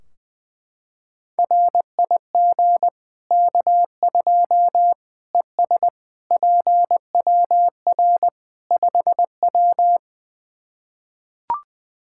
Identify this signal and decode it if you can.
Morse code, heard as a keyed tone, about 20 wpm, 700 Hz. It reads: RIG K2 ES PWR 5W
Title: Practice QSO - Exchange 2 - Info - 20wpm